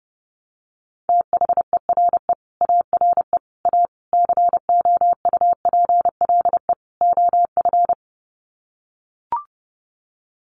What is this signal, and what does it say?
Morse code, heard as a keyed tone, about 30 wpm, 700 Hz. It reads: THERE ARE A COUPLE OF
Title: Sets of 5 Words Encore 30wpm